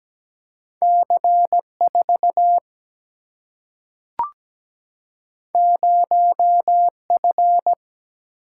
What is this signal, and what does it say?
Morse code, heard as a keyed tone, about 17 wpm, 700 Hz. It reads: C 4 0 F
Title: Single-Letter Single-Number 17wpm